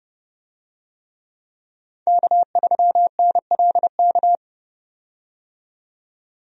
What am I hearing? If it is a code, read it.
Morse code, heard as a keyed tone, about 30 wpm, 700 Hz. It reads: K3NLK
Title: USA Call Signs - 5 Characters - 30wpm